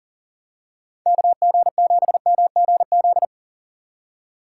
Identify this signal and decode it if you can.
Morse code, heard as a keyed tone, about 40 wpm, 700 Hz. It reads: KG7MGZ
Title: USA Call Signs - 6 Characters - 4x Character Spacing - 40wpm